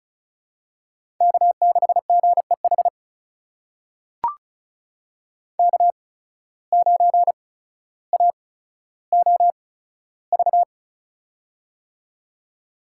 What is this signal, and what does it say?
Morse code, heard as a keyed tone, about 34 wpm, spaced as for 15 wpm, 700 Hz. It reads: K6GEH K9AOV
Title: USA Call Signs - 5 Characters - 8x Character Spacing - 35wpm